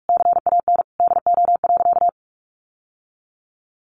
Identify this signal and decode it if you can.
Morse code, heard as a keyed tone, about 44 wpm, 700 Hz. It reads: CAN DO.
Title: Sentences from Top 900 Words 45wpm